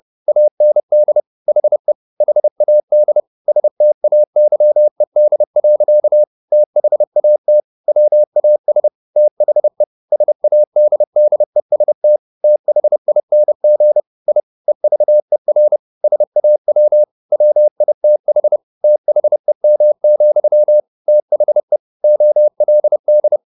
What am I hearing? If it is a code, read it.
Morse code, heard as a keyed tone, about 30 wpm, 600 Hz. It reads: AND HE HAD STAYED. THAT WAS THE SADDEST THING I EVER SAW WITH THEM, THE OLD
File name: the old man and the sea / the old man and the sea 1 30wpm